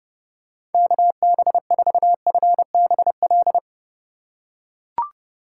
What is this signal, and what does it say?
Morse code, heard as a keyed tone, about 31 wpm, 700 Hz. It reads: KB4FBL
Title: USA Call Signs - 6 Characters - 4x Character Spacing - 30wpm